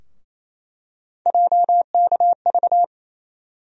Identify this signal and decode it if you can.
Morse code, heard as a keyed tone, about 28 wpm, 700 Hz. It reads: JKV
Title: USA Call Sign Postfixes - 3 Letters 28wpm